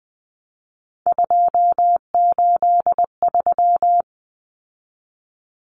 Morse code, heard as a keyed tone, about 20 wpm, 700 Hz. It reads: 283
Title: Common Cut Numbers - 3-Digit 20wpm